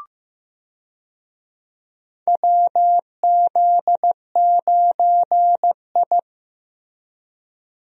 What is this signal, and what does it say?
Morse code, heard as a keyed tone, about 16 wpm, 700 Hz. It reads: WZ9I
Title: USA Call Signs - 4 Characters - Repeated 3x - 15wpm